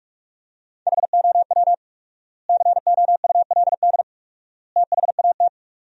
Morse code, heard as a keyed tone, about 46 wpm, 700 Hz. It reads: HOW COULD THAT
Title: 4x Word-Spacing - Sentences from Top 400 Words - 45wpm